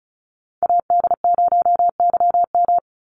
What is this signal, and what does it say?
Morse code, heard as a keyed tone, about 34 wpm, 700 Hz. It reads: AD0YM
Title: NA SOTA Chaser Callsigns 35wpm